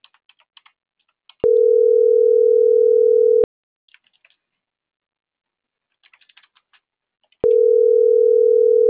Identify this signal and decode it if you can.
Ringback tone (North America), 440 + 480 Hz.